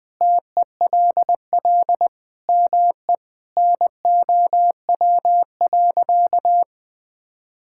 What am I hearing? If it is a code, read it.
Morse code, heard as a keyed tone, about 20 wpm, 700 Hz. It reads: TELL ME NOW.
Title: Sentences with 4-Word Phrases - Spoken Beginning - 20wpm